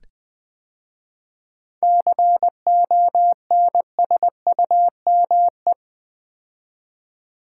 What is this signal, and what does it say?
Morse code, heard as a keyed tone, about 20 wpm, 700 Hz. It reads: CONSUME
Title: 7 Letter Words - First 3 Letters Given - 20wpm